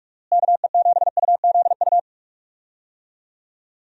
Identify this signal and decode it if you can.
Morse code, heard as a keyed tone, about 47 wpm, 700 Hz. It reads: KE7UZU
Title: USA Call Signs - 6 Characters - Repeated 3x - 45wpm